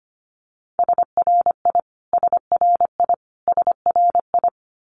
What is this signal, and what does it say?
Morse code, heard as a keyed tone, about 25 wpm, 700 Hz. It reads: SRI SRI SRI
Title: Abbreviations Repeated 3x - 25wpm